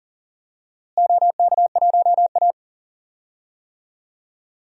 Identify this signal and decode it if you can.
Morse code, heard as a keyed tone, about 40 wpm, 700 Hz. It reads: OK1A
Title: DXCC-100 Call Signs - 4 Characters - Repeated 2x - 40wpm